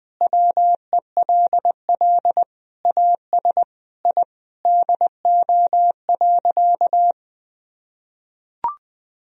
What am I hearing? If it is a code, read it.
Morse code, heard as a keyed tone, about 20 wpm, 700 Hz. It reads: WELL AS I DO.
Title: Sentences from Top 1000 Words - Review Part 3 - 20wpm